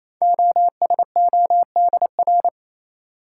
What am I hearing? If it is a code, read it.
Morse code, heard as a keyed tone, about 28 wpm, 700 Hz. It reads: OSODR
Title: Code Groups - 5 Letters - 28wpm